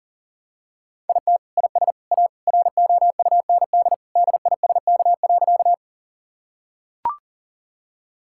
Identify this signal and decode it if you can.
Morse code, heard as a keyed tone, about 37 wpm, 700 Hz. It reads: IT IS A ROUND DISK.
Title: Sentences from Top 1000 Words - Review Part 6 - 40wpm